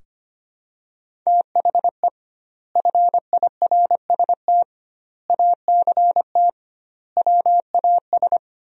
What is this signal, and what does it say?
Morse code, heard as a keyed tone, about 25 wpm, 700 Hz. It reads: THE FIRST ACT WAS
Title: 2x Word-Spacing - Sentences from Top 200 Words - 25wpm